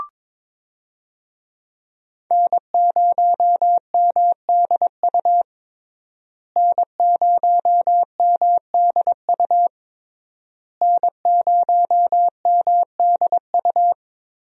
Morse code, heard as a keyed tone, about 22 wpm, 700 Hz. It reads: N0MDU N0MDU N0MDU
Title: USA Call Signs - 5 Characters - Repeated 3x - 22wpm